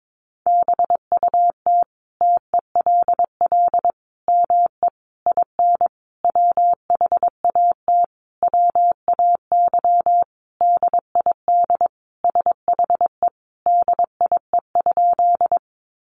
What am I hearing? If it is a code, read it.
Morse code, heard as a keyed tone, about 22 wpm, 700 Hz. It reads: BUT TELL ME IN WHAT WAY DID SHE DIE?
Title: Sentences from Top 1300 Words 22wpm